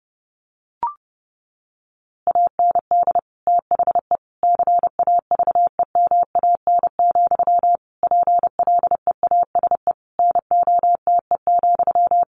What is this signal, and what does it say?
Morse code, heard as a keyed tone, about 28 wpm, 700 Hz. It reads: AND THE CAVEMAN, PLEASE NOTE,
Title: A Sound Of Thunder 30wpm